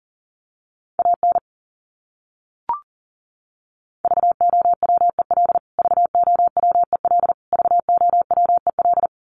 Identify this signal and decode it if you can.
Morse code, heard as a keyed tone, about 40 wpm, 700 Hz. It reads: AN VOWEL VOWEL VOWEL
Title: Top 500 Words Repeated 40wpm